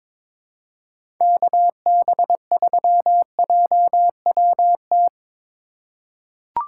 Morse code, heard as a keyed tone, about 22 wpm, 700 Hz. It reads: KB3JWT
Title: USA Call Signs - 6 Characters - 3x Character Spacing - 22wpm